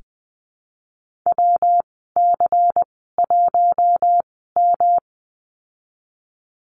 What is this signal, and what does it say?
Morse code, heard as a keyed tone, about 20 wpm, spaced as for 15 wpm, 700 Hz. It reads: WC1M
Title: 173 Lesson 28g - Number 9 - Call Signs 20wpm